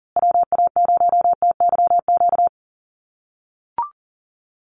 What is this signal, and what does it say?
Morse code, heard as a keyed tone, about 40 wpm, 700 Hz. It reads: WA0TYQ
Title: USA Call Signs Repeated 2x 40wpm